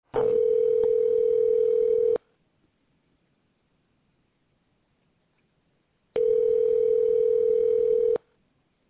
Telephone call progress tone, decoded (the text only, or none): ringback tone